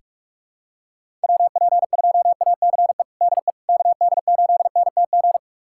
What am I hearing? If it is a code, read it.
Morse code, heard as a keyed tone, about 43 wpm, 700 Hz. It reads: WP2ACE DE KD8NTG
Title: USA Callsign DE Callsign - 6 Characters - 45wpm